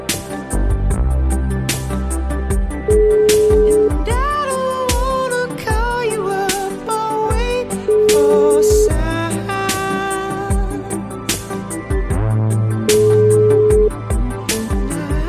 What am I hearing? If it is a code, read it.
Ringback tone (Europe, 425 Hz), with noise.